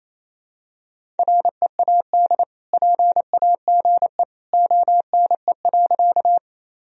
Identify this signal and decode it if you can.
Morse code, heard as a keyed tone, about 28 wpm, 700 Hz. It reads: READ PAGE ONE.
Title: Sentences from Top 1000 Words - Review Part 1 - 28wpm